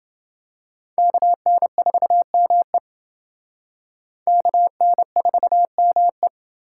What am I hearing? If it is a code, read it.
Morse code, heard as a keyed tone, about 28 wpm, 700 Hz. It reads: KN4ME KN4ME